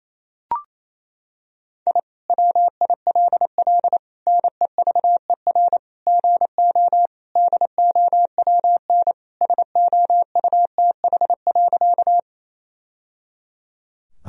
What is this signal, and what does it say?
Morse code, heard as a keyed tone, about 27 wpm, 700 Hz. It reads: I WILL NEVER GO DOWN SOUTH.
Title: Sentences from Top 1000 Words - Review Part 2 - 28wpm